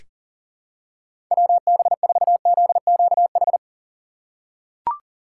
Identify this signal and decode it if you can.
Morse code, heard as a keyed tone, about 39 wpm, 700 Hz. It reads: WB4ZQH
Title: USA Call Signs - 6 Characters - 2x Character Spacing - 40wpm